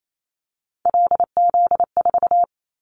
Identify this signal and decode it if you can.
Morse code, heard as a keyed tone, about 28 wpm, 700 Hz. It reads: LZ4
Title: Callsign Prefixes for Top 25 DXCC - 28wpm